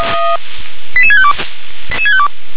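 Five-tone selective call (ZVEI), heard as quick runs of five tones, tones of about 70 ms. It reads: 88652
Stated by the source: SDR-received ZVEI five-tone call.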